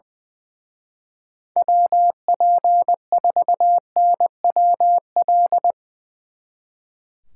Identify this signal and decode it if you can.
Morse code, heard as a keyed tone, about 23 wpm, 700 Hz. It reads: WP4NWL
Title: Speed Racing - USA 2x3 Call Signs 20wpm